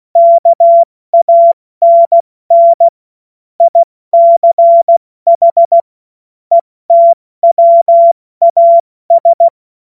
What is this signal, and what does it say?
Morse code, heard as a keyed tone, about 16 wpm, 680 Hz. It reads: KANN ICH ETWAS
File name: Morsekurs / Witze / Witz 06. Dezember 2022 70 F80